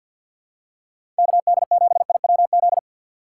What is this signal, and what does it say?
Morse code, heard as a keyed tone, about 53 wpm, 700 Hz. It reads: KD7IWZ
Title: USA Call Signs - 6 Characters - 4x Character Spacing - 50wpm